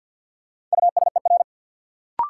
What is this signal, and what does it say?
Morse code, heard as a keyed tone, about 46 wpm, 700 Hz. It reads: USER
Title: Sentences from Top 1000 Words - Review Part 5 - New Word Drill - 50wpm